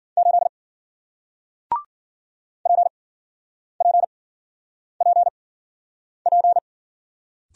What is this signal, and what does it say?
Morse code, heard as a keyed tone, about 56 wpm, spaced as for 9 wpm, 700 Hz. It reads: 8 PPPP